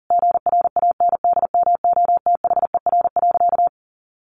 Morse code, heard as a keyed tone, about 40 wpm, 700 Hz. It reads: GRANDMOTHER.
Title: Sentences from Top 1200 Words 40wpm